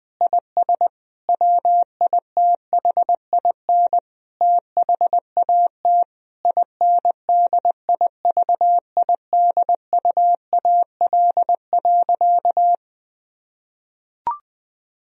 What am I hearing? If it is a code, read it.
Morse code, heard as a keyed tone, about 21 wpm, 700 Hz. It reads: IS WITHIN THAT INDIVIDUAL.
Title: Sentences from Top 1000 Words - Review Part 4 - 20wpm